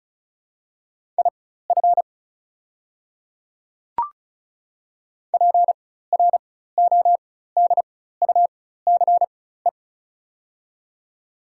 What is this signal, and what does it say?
Morse code, heard as a keyed tone, about 36 wpm, spaced as for 16 wpm, 690 Hz. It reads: IF PRODUCE
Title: 4x Character Spacing - Top 500 Words - 35wpm